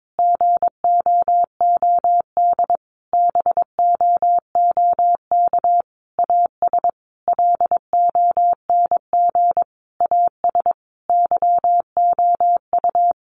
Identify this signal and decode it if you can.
Morse code, heard as a keyed tone, about 22 wpm, 700 Hz. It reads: GOOD BOOK AS LONG AS YOU
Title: Sentences from Top 1200 Words 22wpm